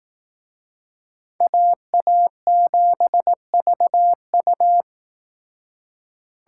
Morse code, heard as a keyed tone, about 19 wpm, 700 Hz. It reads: AA7VU
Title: Speed Racing - USA 2x2 Call Signs 15wpm